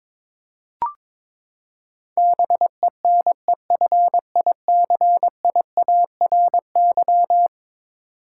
Morse code, heard as a keyed tone, about 21 wpm, 700 Hz. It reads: BENEFICIARY
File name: practice / 22wpm / Eleven Letter Words - 22wpm